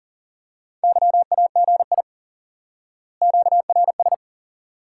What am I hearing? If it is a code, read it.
Morse code, heard as a keyed tone, about 40 wpm, 700 Hz. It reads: YAGI QRS
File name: IQE Course 3 - Lesson 89 - YAGI - Two Elements - 40wpm